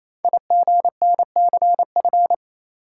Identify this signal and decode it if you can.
Morse code, heard as a keyed tone, about 27 wpm, 700 Hz. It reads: IGNCF
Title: Code Groups - 5 Letters - 28wpm